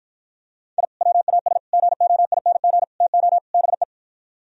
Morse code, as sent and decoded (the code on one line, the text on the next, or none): .. / .-- .- ... / --. --- .. -. --. / - --- / -... .
I WAS GOING TO BE